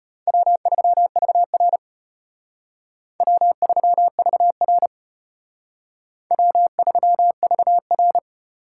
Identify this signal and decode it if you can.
Morse code, heard as a keyed tone, about 33 wpm, 700 Hz. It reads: W3VR W3VR W3VR